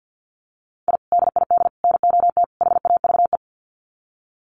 Morse code, heard as a keyed tone, about 50 wpm, 700 Hz. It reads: I DID NOT HAVE
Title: Sets of 4 Words 50wpm